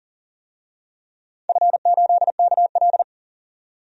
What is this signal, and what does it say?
Morse code, heard as a keyed tone, about 40 wpm, 700 Hz. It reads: F8KL
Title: DXCC-100 Call Signs - 4 Characters - 40wpm